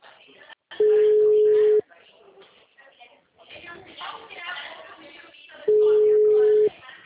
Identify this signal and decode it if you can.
Ringback tone (Europe, 425 Hz), with speech.